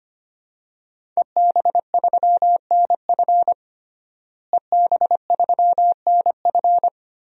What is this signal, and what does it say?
Morse code, heard as a keyed tone, about 25 wpm, 700 Hz. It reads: EB3NF EB3NF
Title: DXCC-100 Call Signs - 5 Characters - Repeated 3x - 25wpm